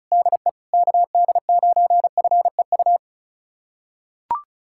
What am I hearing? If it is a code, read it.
Morse code, heard as a keyed tone, about 34 wpm, 700 Hz. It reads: DE KD9FEU
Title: USA Callsign DE Callsign - 6 Characters - 35wpm